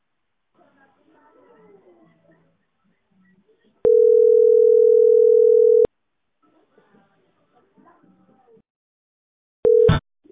Ringback tone (North America), 440 + 480 Hz.